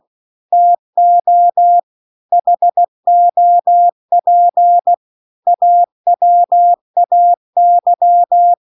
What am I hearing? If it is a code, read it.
Morse code, heard as a keyed tone, about 16 wpm, 700 Hz. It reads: TO HOP AWAY